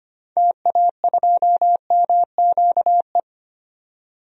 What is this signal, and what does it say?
Morse code, heard as a keyed tone, about 25 wpm, 700 Hz. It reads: TA2MQE
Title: DXCC-100 Call Signs - 6 Characters - 25wpm